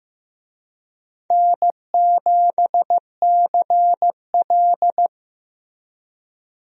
Morse code, heard as a keyed tone, about 15 wpm, 700 Hz. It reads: N7CL
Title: USA Call Signs - 4 Characters - Repeated 3x - 15wpm